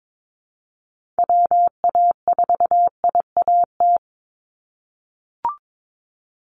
Morse code, heard as a keyed tone, about 22 wpm, 700 Hz. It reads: WA4IAT
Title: USA Call Signs 22wpm